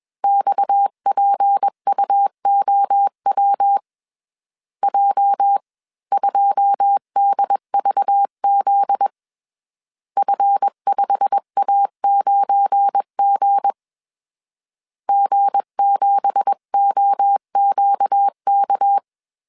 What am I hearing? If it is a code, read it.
Morse code, heard as a keyed tone, about 21 wpm, 800 Hz. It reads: XPUOW J2DVZ F5A9G G7OQK
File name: CW / DET / 1 9 1500C S C 20wpm 2310